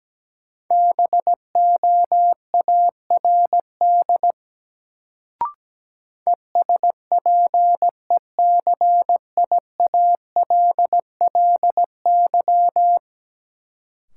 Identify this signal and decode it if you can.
Morse code, heard as a keyed tone, about 17 wpm, 700 Hz. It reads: BOARD ESPECIALLY
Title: Sentences from Top 700 Words - New Word Drill - 17wpm